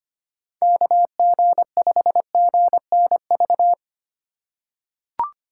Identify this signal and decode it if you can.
Morse code, heard as a keyed tone, about 25 wpm, 700 Hz. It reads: KG5GNV
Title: USA Call Signs - 6 Characters - 2x Character Spacing - 25wpm